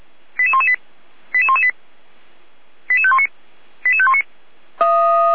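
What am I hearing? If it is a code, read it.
Five-tone selective call (ZVEI), heard as quick runs of five tones, tones of about 70 ms. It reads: 88118 88118 88519 88519
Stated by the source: ZVEI five-tone selective call 88129, received on SDR.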